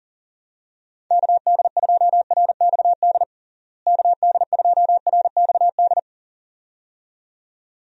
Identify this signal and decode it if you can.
Morse code, heard as a keyed tone, about 40 wpm, 700 Hz. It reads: KD2RXD KD2RXD
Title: USA Call Signs - 6 Characters - Repeated 2x - 40wpm